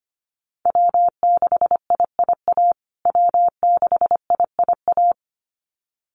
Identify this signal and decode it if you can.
Morse code, heard as a keyed tone, about 25 wpm, 700 Hz. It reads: W6IIA W6IIA
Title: USA Call Signs Repeated 2x 25wpm